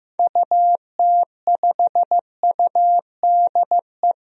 Morse code, heard as a keyed tone, about 15 wpm, 700 Hz. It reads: UT5UDE
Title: DXCC-100 Call Signs - 6 Characters - 15wpm